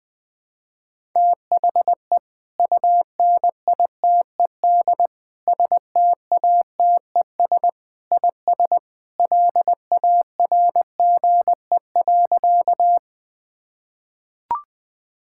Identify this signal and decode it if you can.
Morse code, heard as a keyed tone, about 20 wpm, 700 Hz. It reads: THE UNITED STATES IS LARGE.
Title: Sentences with 3-Word Phrases - 20wpm